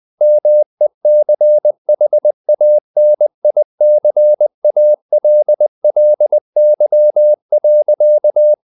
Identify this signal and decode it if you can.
Morse code, heard as a keyed tone, about 20 wpm, 600 Hz. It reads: MECHANICALLY.
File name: the old man and the sea / the old man and the sea part 1 20wpm